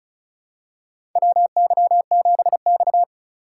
Morse code, heard as a keyed tone, about 36 wpm, 700 Hz. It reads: WY7X
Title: USA Call Signs - 4 Characters - 2x Character Spacing - 35wpm